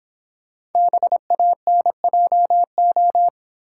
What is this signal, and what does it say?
Morse code, heard as a keyed tone, about 26 wpm, 700 Hz. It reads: BANJO